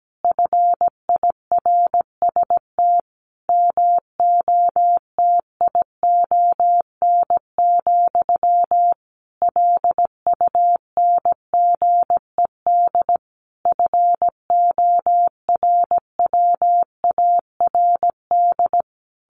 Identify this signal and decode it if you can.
Morse code, heard as a keyed tone, about 17 wpm, 700 Hz. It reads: FIRST MOTION, LUNGED FORWARD
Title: A Sound Of Thunder 17wpm